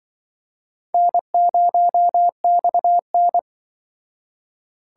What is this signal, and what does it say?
Morse code, heard as a keyed tone, about 25 wpm, 700 Hz. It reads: N0XN
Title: Speed Racing - USA 1x2 Call Signs 20wpm